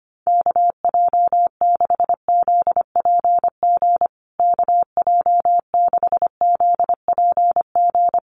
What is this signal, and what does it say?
Morse code, heard as a keyed tone, about 25 wpm, 700 Hz. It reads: KJ6ZPG KJ6ZPG
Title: USA Call Signs Repeated 3x 25wpm